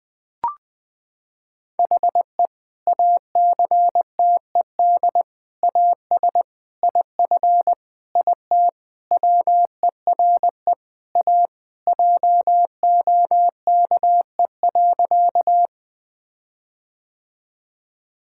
Morse code, heard as a keyed tone, about 20 wpm, 700 Hz. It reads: HE ACTED AS IF IT WERE A JOKE.
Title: Sentences with 5-Word Phrases - 20wpm